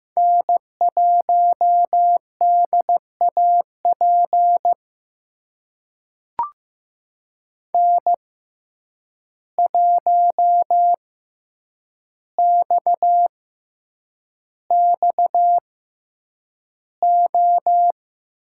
Morse code, heard as a keyed tone, about 15 wpm, spaced as for 8 wpm, 700 Hz. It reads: N1DAP N1XXO
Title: USA Call Signs - 5 Characters - 6x Character Spacing - 15wpm